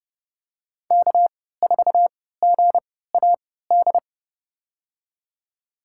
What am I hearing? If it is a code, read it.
Morse code, heard as a keyed tone, about 30 wpm, spaced as for 17 wpm, 700 Hz. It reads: K4GAD
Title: USA Call Signs - 5 Characters - 3x Character Spacing - 30wpm